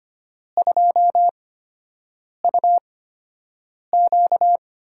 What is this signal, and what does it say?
Morse code, heard as a keyed tone, about 25 wpm, spaced as for 7 wpm, 700 Hz. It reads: 2UQ
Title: USA Call Signs - 8x Character Spacing - 25wpm